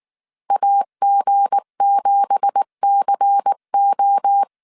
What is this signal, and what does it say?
Morse code, heard as a keyed tone, about 19 wpm, 800 Hz. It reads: AG7CO